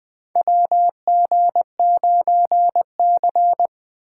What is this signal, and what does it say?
Morse code, heard as a keyed tone, about 20 wpm, 700 Hz. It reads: WG9C
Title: USA Call Signs - 4 Characters - 3x Character Spacing - 20wpm